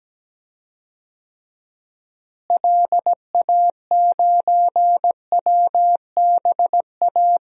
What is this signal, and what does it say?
Morse code, heard as a keyed tone, about 17 wpm, 700 Hz. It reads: LA9WBA
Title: DXCC-100 Call Signs - 6 Characters - Repeated 3x - 17wpm